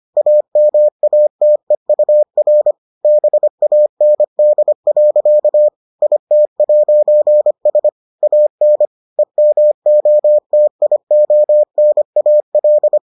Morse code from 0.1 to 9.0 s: AMATEUR BAND. IT'S AN

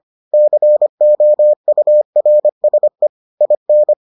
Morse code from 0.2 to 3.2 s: COURSE

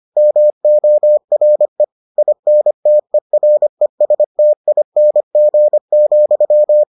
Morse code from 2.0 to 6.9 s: INTERESTING,

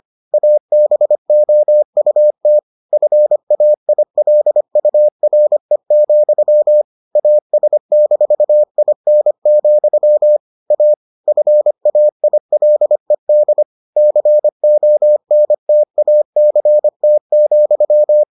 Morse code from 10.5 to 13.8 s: A FAILED

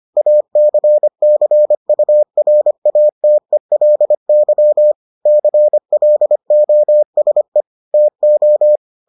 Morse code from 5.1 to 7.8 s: CLOSE

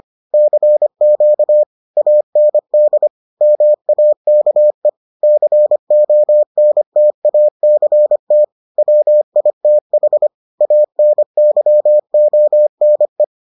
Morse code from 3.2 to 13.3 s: MAKE CONTACT WITH ANYONE

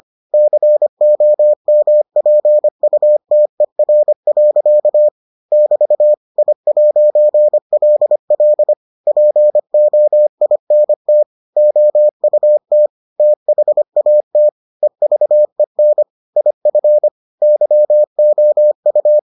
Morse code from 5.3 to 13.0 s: <BT> I'LL POINT OUT

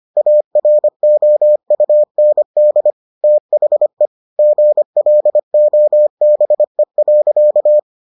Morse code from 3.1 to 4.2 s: THE

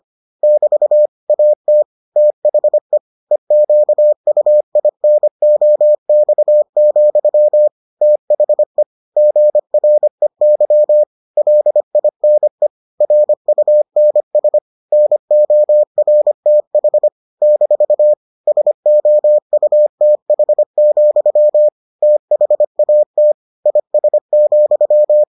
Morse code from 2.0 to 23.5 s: THE EQUINOX, THE GREY LINE RUNS NORTH - SOUTH, THAT

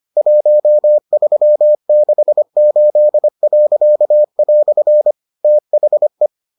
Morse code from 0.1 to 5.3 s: 1368."

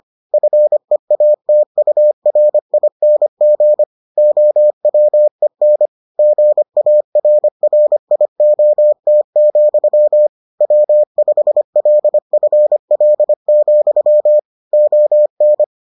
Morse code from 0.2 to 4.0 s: FEATURING

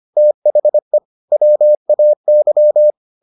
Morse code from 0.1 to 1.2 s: THE

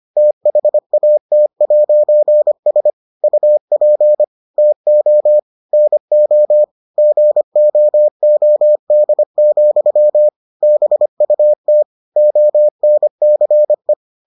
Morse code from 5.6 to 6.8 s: NO

